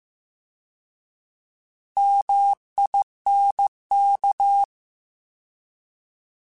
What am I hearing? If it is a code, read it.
Morse code, heard as a keyed tone, about 15 wpm, 800 Hz. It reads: MINK
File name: Alphabetical / Alphabetical mp3s / Ms @13